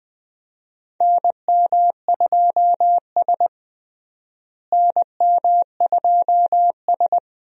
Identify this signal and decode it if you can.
Morse code, heard as a keyed tone, about 20 wpm, 700 Hz. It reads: NM2S NM2S
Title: USA Call Signs - 4 Characters - Repeated 3x - 20wpm